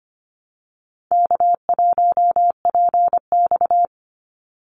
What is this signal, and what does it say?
Morse code, heard as a keyed tone, about 25 wpm, 700 Hz. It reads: K1PX
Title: Easy Call Signs Repeated 3x - 25wpm